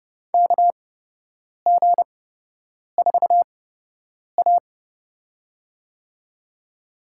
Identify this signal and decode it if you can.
Morse code, heard as a keyed tone, about 30 wpm, spaced as for 8 wpm, 700 Hz. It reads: KG4A
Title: USA Call Signs - 4 Characters - 8x Character Spacing - 30wpm